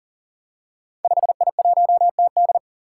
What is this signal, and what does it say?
Morse code, heard as a keyed tone, about 42 wpm, 700 Hz. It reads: 5I1TD
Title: Five Challenging Letter-Number Combinations 40wpm